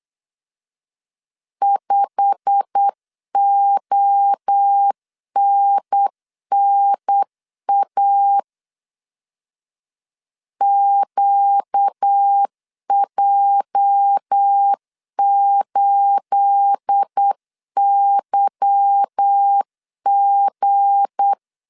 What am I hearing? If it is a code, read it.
Morse code, heard as a keyed tone, about 8 wpm, 800 Hz. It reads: 5ONNA QJ8YG
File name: CW / DET / 1 9 1500C S A 8wpm 5637